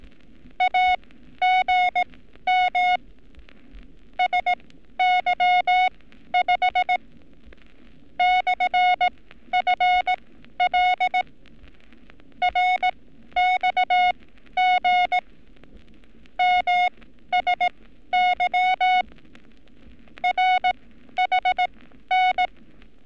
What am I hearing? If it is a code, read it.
Morse code, heard as a keyed tone, about 18 wpm, spaced as for 12 wpm, 730 Hz. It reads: AGM SY5 /FL RXG MSY RHN
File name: training / NAVPERS 11224-11235 / NAVPERS 11227 RA RB